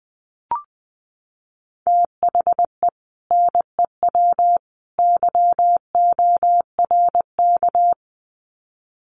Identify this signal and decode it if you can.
Morse code, heard as a keyed tone, about 20 wpm, 700 Hz. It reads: THE NEW YORK
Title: Sets of 3 Words 20wpm